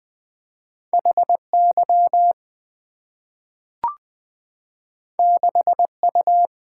Morse code, heard as a keyed tone, about 20 wpm, 690 Hz. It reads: HY 6U
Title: Two Letter-Number Combinations - 20wpm